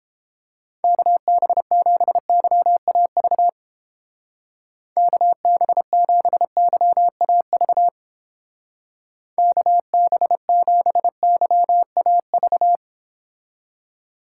Morse code, heard as a keyed tone, about 30 wpm, 700 Hz. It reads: KB7YAV KB7YAV KB7YAV